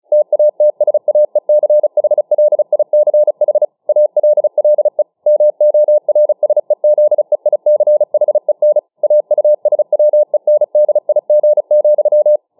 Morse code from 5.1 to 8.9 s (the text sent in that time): MORSEZEICHEN